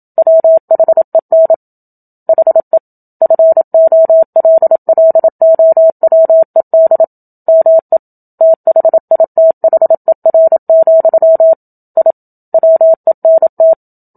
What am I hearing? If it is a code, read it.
Morse code, heard as a keyed tone, about 27 wpm, 650 Hz. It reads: WHEN HE FOLLOWED ME THITHER, I WENT
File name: Book Two / 01-Chapter 1 - 27 WPM